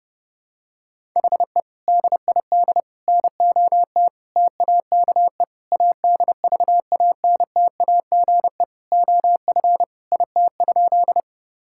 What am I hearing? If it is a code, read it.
Morse code, heard as a keyed tone, about 32 wpm, 700 Hz. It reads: HE DID NOT TAKE ADVANTAGE OF IT?
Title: Sentences from Top 1000 Words - Review Part 6 - 30wpm